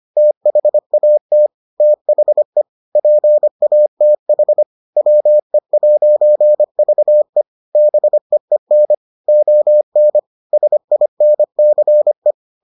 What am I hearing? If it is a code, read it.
Morse code, heard as a keyed tone, about 25 wpm, 600 Hz. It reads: THAT THE PATH WE'VE BEEN ON SINCE